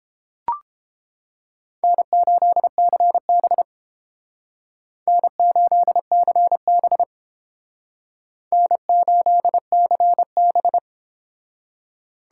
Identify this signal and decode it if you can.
Morse code, heard as a keyed tone, about 28 wpm, 700 Hz. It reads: N8CB N8CB N8CB